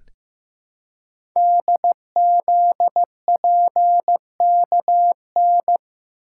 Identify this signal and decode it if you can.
Morse code, heard as a keyed tone, about 15 wpm, 700 Hz. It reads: DZPKN
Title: Code Groups - 5 Letters - 15wpm